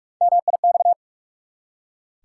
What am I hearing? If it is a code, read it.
Morse code, heard as a keyed tone, about 44 wpm, 700 Hz. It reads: MIX
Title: Three Letter Words - 45wpm